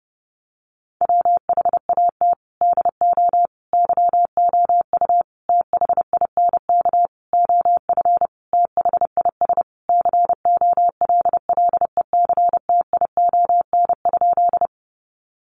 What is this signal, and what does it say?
Morse code, heard as a keyed tone, about 30 wpm, 700 Hz. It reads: WHAT DO YOU THINK OF THIS COLLECTION?
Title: Sentences from Top 1200 Words 30wpm